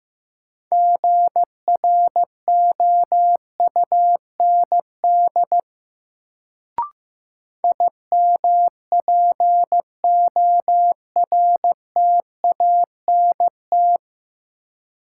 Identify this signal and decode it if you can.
Morse code, heard as a keyed tone, about 15 wpm, 700 Hz. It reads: GROUND IMPORTANT
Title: Sets of 3 Words - New Word Drill - 15wpm